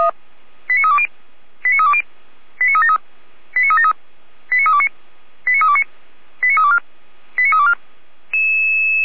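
Five-tone selective call (ZVEI), heard as quick runs of five tones, tones of about 70 ms. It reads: 89320 89320 89373 89373 89329 89329 89324 89324
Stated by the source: ZVEI five-tone selective call received by SDR, from the feuerwehr-orlamuende.org fire brigade audio server.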